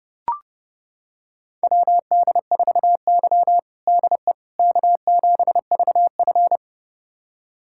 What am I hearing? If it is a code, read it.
Morse code, heard as a keyed tone, about 30 wpm, 700 Hz. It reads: WD4Y DE K7VF
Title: USA Callsign DE Callsign - 4 Characters - 30wpm